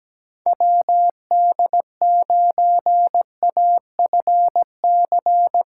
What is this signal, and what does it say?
Morse code, heard as a keyed tone, about 19 wpm, 700 Hz. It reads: WD9AFC
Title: Speed Racing - USA 2x3 Call Signs 17wpm